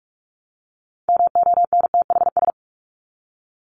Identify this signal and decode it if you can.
Morse code, heard as a keyed tone, about 44 wpm, 700 Hz. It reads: MONTHS
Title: ES:S Suffix - Top 100 Words with Base Words 45wpm